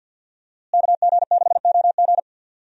Morse code, heard as a keyed tone, about 50 wpm, 700 Hz. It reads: KG6OG
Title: USA Call Signs - 5 Characters - Repeated 2x - 50wpm